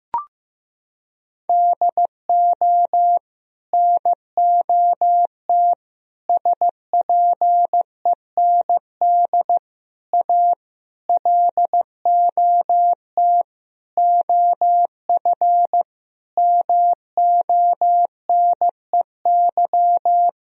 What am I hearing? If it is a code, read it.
Morse code, heard as a keyed tone, about 16 wpm, 700 Hz. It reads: DO NOT SPEND A LOT OF MONEY
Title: Sentences from Top 1000 Words - Review Part 4 - 15wpm